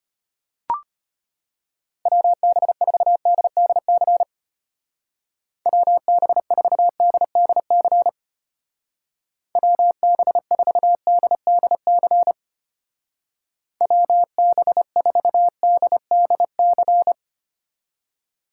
Morse code, heard as a keyed tone, about 32 wpm, 700 Hz. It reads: WB4DDC WB4DDC WB4DDC WB4DDC